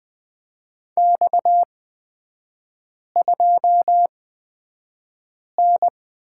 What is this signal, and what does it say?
Morse code, heard as a keyed tone, about 20 wpm, spaced as for 5 wpm, 690 Hz. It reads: X2N
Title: 229 Lesson 37e - Letter x - Three Characters 20wpm